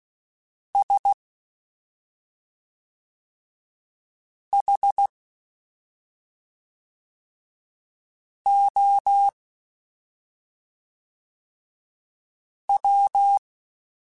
Morse code, heard as a keyed tone, about 16 wpm, 800 Hz. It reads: S H O W
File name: Special INGs / Special INGs-mp3s / SlowToFaster ing @5